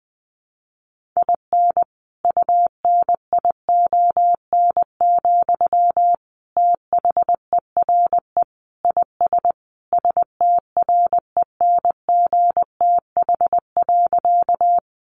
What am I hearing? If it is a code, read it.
Morse code, heard as a keyed tone, about 20 wpm, 700 Hz. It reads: IN UNION, THERE IS STRENGTH.
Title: Sentences from Top 1400 Words 20wpm